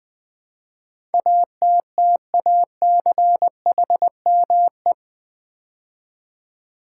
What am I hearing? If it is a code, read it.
Morse code, heard as a keyed tone, about 20 wpm, 700 Hz. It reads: ATTACHME
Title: Build A Word - 4 Letters At A Time - 10-Letter Words - 20wpm